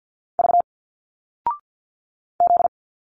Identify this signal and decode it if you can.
Morse code, heard as a keyed tone, about 48 wpm, 700 Hz. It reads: V Z
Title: Single Letters 50wpm